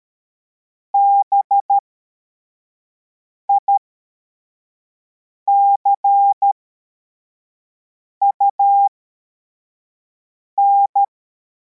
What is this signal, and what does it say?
Morse code, heard as a keyed tone, about 13 wpm, spaced as for 4 wpm, 800 Hz. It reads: BICUN